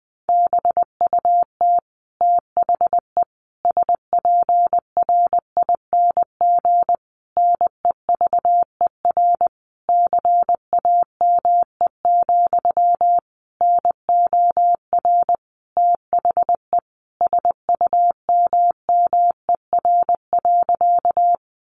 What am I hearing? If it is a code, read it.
Morse code, heard as a keyed tone, about 20 wpm, 700 Hz. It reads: BUT THE SPRING NEVER CAME, NOR THE SUMMER.
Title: Sentences from Top 1500 Words 20wpm